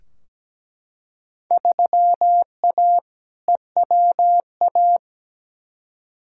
Morse code, heard as a keyed tone, about 17 wpm, 700 Hz. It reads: 3A EWA
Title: Field Day Exchanges - 17wpm